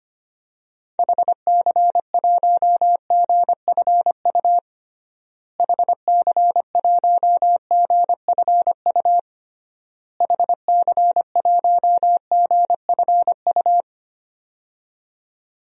Morse code, heard as a keyed tone, about 25 wpm, 690 Hz. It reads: HC1GFU HC1GFU HC1GFU
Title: DXCC-100 Call Signs - 6 Characters - Repeated 3x - 25wpm